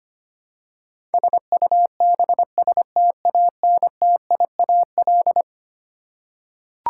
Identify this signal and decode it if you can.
Morse code, heard as a keyed tone, about 25 wpm, 700 Hz. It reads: SUBSTANTIAL
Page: morse-code-ninja-practice-25wpm